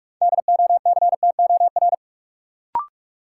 Morse code, heard as a keyed tone, about 42 wpm, 700 Hz. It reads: DOCTOR
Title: Sentences from Top 1500 Words - New Word Drill - 45wpm